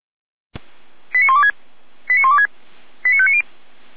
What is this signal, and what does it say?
Five-tone selective call (ZVEI), heard as quick runs of five tones, tones of about 70 ms. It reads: 89126 89126 89590
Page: SDR/ZVEI